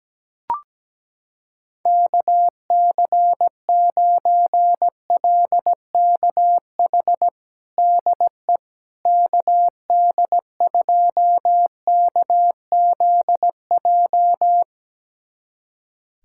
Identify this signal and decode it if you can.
Morse code, heard as a keyed tone, about 17 wpm, 700 Hz. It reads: KC9LKH DE KD2KZJ
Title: USA Callsign DE Callsign 17wpm